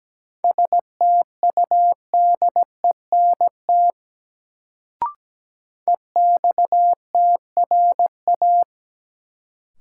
Morse code, heard as a keyed tone, about 17 wpm, 700 Hz. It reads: STUDENT EXTRA
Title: Sentences from Top 1000 Words - New Word Drill - 17wpm